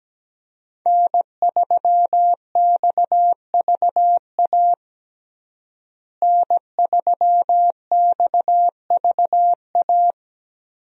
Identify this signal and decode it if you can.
Morse code, heard as a keyed tone, about 17 wpm, 700 Hz. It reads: N3XVA N3XVA
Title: USA Call Signs - 5 Characters - Repeated 3x - 17wpm